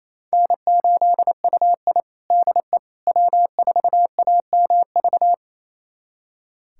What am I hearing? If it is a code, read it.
Morse code, heard as a keyed tone, about 28 wpm, 700 Hz. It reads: N8UI DE W4AMV
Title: USA Callsign DE Callsign 28wpm